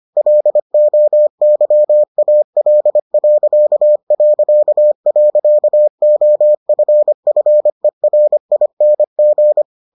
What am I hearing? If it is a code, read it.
Morse code, heard as a keyed tone, about 25 wpm, 600 Hz. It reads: LOYAL...OFFERING